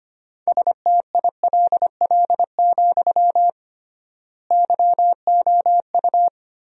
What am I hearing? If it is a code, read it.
Morse code, heard as a keyed tone, about 25 wpm, 700 Hz. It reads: STILL, YOU
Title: 3x Word-Spacing - Sentences from Top 300 Words - 25wpm